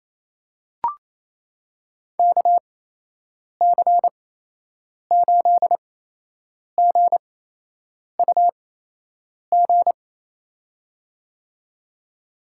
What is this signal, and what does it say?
Morse code, heard as a keyed tone, about 28 wpm, spaced as for 8 wpm, 700 Hz. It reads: KC8GUG
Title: USA Call Signs - 6 Characters - 8x Character Spacing - 28wpm